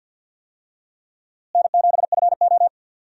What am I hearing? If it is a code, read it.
Morse code, heard as a keyed tone, about 51 wpm, 700 Hz. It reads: N7FO
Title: USA Call Signs - 4 Characters - Repeated 2x - 50wpm